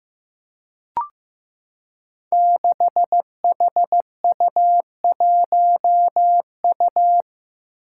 Morse code, heard as a keyed tone, about 15 wpm, 700 Hz. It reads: 6HU1U
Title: Five Challenging Letter-Number Combinations 15wpm